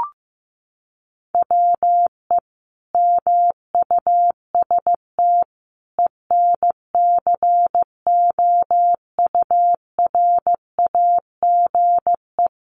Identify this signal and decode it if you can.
Morse code, heard as a keyed tone, about 14 wpm, 700 Hz. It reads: WE MUST ENCOURAGE
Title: Sentences from Top 1400 Words 15wpm